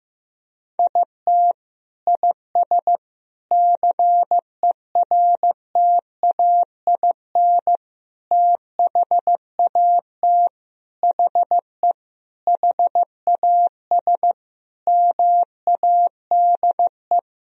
Morse code, heard as a keyed tone, about 16 wpm, 700 Hz. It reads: IT IS CERTAIN THAT HE HAS MADE
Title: Sentences from Top 1000 Words - Review Part 2 - 15wpm